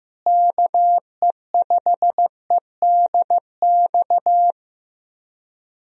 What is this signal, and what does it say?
Morse code, heard as a keyed tone, about 15 wpm, 700 Hz. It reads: KE5EDX
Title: USA Call Signs - 6 Characters - Repeated 2x - 15wpm